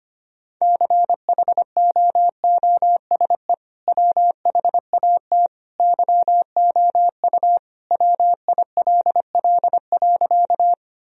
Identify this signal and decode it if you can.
Morse code, heard as a keyed tone, about 25 wpm, 700 Hz. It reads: CHOOSE WHAT YOU WILL.
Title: Sentences from Top 1000 Words - Review Part 4 - 25wpm